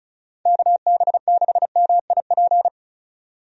Morse code, heard as a keyed tone, about 35 wpm, 700 Hz. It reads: KB6MIP
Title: USA Call Signs - 6 Characters - 35wpm